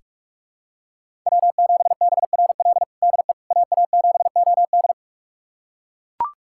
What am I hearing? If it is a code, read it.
Morse code, heard as a keyed tone, about 42 wpm, 700 Hz. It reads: W7BRL DE AA7OD
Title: USA Callsign DE Callsign - 5 Characters - 45wpm